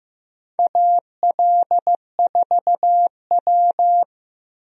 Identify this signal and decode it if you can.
Morse code, heard as a keyed tone, about 15 wpm, 700 Hz. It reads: AL4W
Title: USA Call Signs - 4 Characters - Repeated 2x - 15wpm